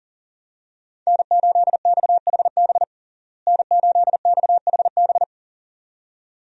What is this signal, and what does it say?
Morse code, heard as a keyed tone, about 40 wpm, 700 Hz. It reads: N8XHB N8XHB
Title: USA Call Signs - 5 Characters - Repeated 2x - 40wpm